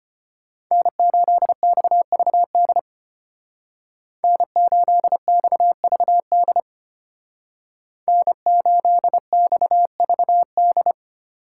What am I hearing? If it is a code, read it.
Morse code, heard as a keyed tone, about 30 wpm, 700 Hz. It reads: N8XVD N8XVD N8XVD
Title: Speed Racing - USA 1x3 Call Signs 25wpm